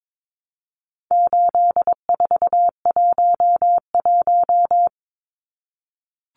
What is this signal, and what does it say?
Morse code, heard as a keyed tone, about 22 wpm, 700 Hz. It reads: 8411
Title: Numbers - 4-Digit 22wpm